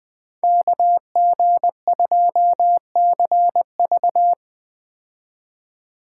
Morse code, heard as a keyed tone, about 20 wpm, 700 Hz. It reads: KG2CV
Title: 2023 POTA Hunter Call Signs - 20wpm